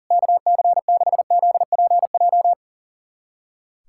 Morse code, heard as a keyed tone, about 37 wpm, 700 Hz. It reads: KC6ZPJ
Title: USA Callsign DE Callsign - 6 Characters - 40wpm